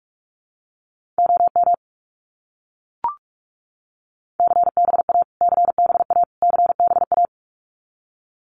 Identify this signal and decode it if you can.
Morse code, heard as a keyed tone, about 44 wpm, 700 Hz. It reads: OM CBA CBA CBA
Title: Abbreviations Repeated 3x - 45wpm